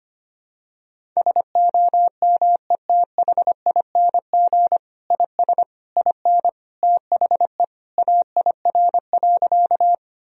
Morse code, heard as a keyed tone, about 25 wpm, 700 Hz. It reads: SOMETHING IS IN THE AIR.
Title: Sentences from Top 1000 Words - Review Part 1 - 25wpm